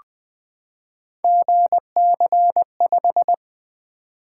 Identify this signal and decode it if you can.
Morse code, heard as a keyed tone, about 20 wpm, 700 Hz. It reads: GC5
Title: Three Letter-Number Combinations - 20wpm